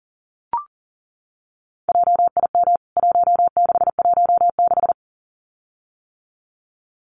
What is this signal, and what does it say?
Morse code, heard as a keyed tone, about 40 wpm, 700 Hz. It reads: JIM 1616
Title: CWT Exchange 40wpm